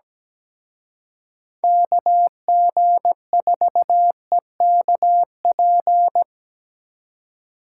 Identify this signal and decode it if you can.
Morse code, heard as a keyed tone, about 20 wpm, 700 Hz. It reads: KG4EKP
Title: Speed Racing - USA 2x3 Call Signs 17wpm